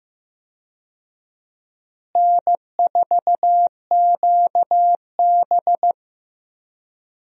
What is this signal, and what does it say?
Morse code, heard as a keyed tone, about 15 wpm, 700 Hz. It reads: N4QB
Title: USA Call Signs - 4 Characters - Repeated 2x - 15wpm